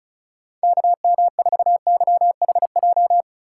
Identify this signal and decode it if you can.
Morse code, heard as a keyed tone, about 36 wpm, 700 Hz. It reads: KM4YHJ
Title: USA Call Signs - 6 Characters - 3x Character Spacing - 35wpm